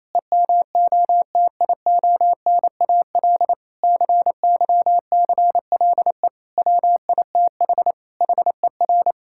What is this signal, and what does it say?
Morse code, heard as a keyed tone, about 27 wpm, 700 Hz. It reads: EMOTIONAL CYCLE WITH HER
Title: Sentences from Top 1000 Words - Review Part 6 - 28wpm